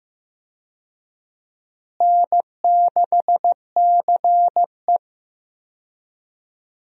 Morse code, heard as a keyed tone, about 15 wpm, 700 Hz. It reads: N6CE
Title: USA Call Signs - 4 Characters - Repeated 3x - 15wpm